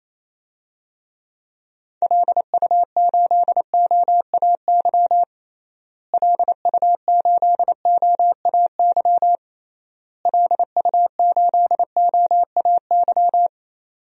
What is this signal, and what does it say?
Morse code, heard as a keyed tone, about 28 wpm, 700 Hz. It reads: LU8OAY LU8OAY LU8OAY
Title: DXCC-100 Call Signs - 6 Characters - Repeated 3x - 28wpm